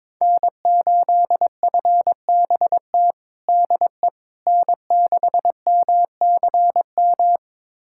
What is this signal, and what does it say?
Morse code, heard as a keyed tone, about 22 wpm, 700 Hz. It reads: N8FBT DE N6MCM
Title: USA Callsign DE Callsign - 5 Characters - 22wpm